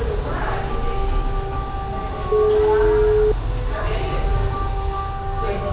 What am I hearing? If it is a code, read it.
Ringback tone (Europe, 425 Hz), with speech.